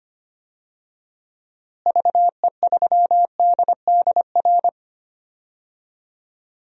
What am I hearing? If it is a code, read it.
Morse code, heard as a keyed tone, about 25 wpm, 700 Hz. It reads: VE3DDR
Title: DXCC-100 Call Signs - 6 Characters - Repeated 3x - 25wpm